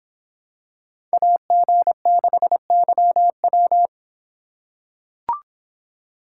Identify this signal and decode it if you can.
Morse code, heard as a keyed tone, about 26 wpm, 700 Hz. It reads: AG6YW